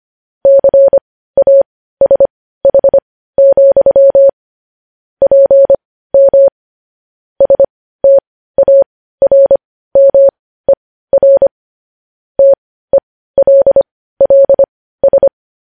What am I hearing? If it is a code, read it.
Morse code, heard as a keyed tone, about 25 wpm, spaced as for 15 wpm, 550 Hz. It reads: CASH, PM STARMER TELLS
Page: news-headlines-in-morse-code-at-15-wpm